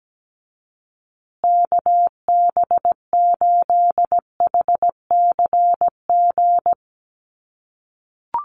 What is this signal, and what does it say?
Morse code, heard as a keyed tone, about 17 wpm, 700 Hz. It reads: KB8HCG
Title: USA Call Signs Repeated 2x 17wpm